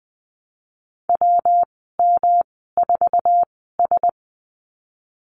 Morse code, heard as a keyed tone, about 20 wpm, spaced as for 15 wpm, 700 Hz. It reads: WM4S
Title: 128 Lesson 21g - Number 6 - Call Signs 20wpm